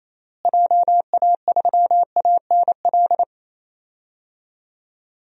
Morse code, heard as a keyed tone, about 27 wpm, 700 Hz. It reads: JA3ANL
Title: DXCC-100 Call Signs - 6 Characters - Repeated 2x - 28wpm